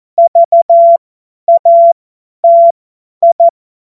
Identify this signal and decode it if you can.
Morse code, heard as a keyed tone, about 14 wpm, spaced as for 10 wpm, 680 Hz. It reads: VATI
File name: Morsekurs / Witze / Witz 27. September 2022 50 F70